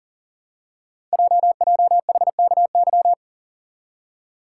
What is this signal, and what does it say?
Morse code, heard as a keyed tone, about 41 wpm, 700 Hz. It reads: JJHKY
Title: Code Groups - 5 Letters - 40wpm